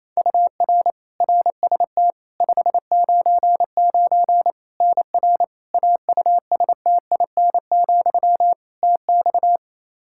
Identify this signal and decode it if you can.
Morse code, heard as a keyed tone, about 28 wpm, 700 Hz. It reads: UR RST 599 NR AUSTIN, TX
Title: Practice QSO - Exchange 1 - Beginning - 28wpm